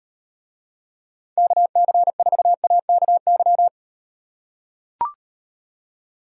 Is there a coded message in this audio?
Yes: Morse code from 1.4 to 3.7 s, heard as a keyed tone, about 38 wpm, 700 Hz. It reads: KC4AKY